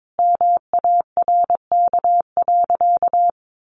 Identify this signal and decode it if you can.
Morse code, heard as a keyed tone, about 22 wpm, 700 Hz. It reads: MARK.
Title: Sentences from Top 1100 Words 22wpm